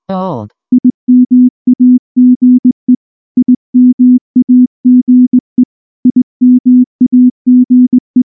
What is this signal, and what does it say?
Morse code, heard as a keyed tone, about 18 wpm, 250 Hz. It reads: IMAGE IMAGE IMAGE
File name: morse / morse-training-20WPM-9